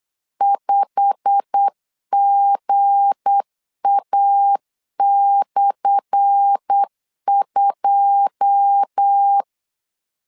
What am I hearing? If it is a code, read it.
Morse code, heard as a keyed tone, about 8 wpm, 800 Hz. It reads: 5GA/2
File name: LEZIONI / 10 SIGNS / 10A SLASH 8wpm 3452